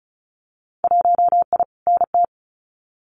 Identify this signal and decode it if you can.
Morse code, heard as a keyed tone, about 35 wpm, 700 Hz. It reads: 1I NT
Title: Winter Field Day Exchanges Encore 35wpm